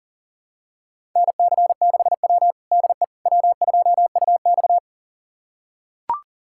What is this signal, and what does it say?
Morse code, heard as a keyed tone, about 40 wpm, 700 Hz. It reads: NC6W DE W2UX
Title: USA Callsign DE Callsign - 4 Characters - 40wpm